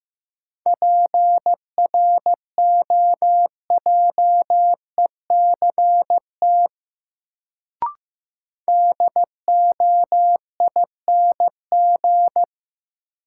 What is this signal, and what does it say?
Morse code, heard as a keyed tone, about 15 wpm, 700 Hz. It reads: PROJECT DOING
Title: Sentences from Top 1400 Words - New Word Drill - 15wpm